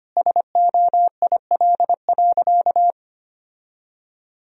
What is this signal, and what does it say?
Morse code, heard as a keyed tone, about 25 wpm, 700 Hz. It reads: SOIL.
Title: Sentences from Top 1000 Words - Review Part 4 - 25wpm